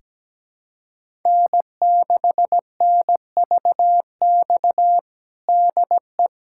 Morse code, heard as a keyed tone, about 17 wpm, 700 Hz. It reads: N6NVX DE
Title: USA Callsign DE Callsign - 5 Characters - 17wpm